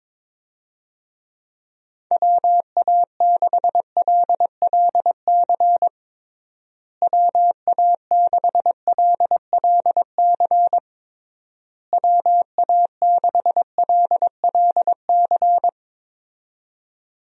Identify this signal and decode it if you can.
Morse code, heard as a keyed tone, about 22 wpm, 700 Hz. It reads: WA6LLC WA6LLC WA6LLC
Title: USA Call Signs - 6 Characters - Repeated 3x - 22wpm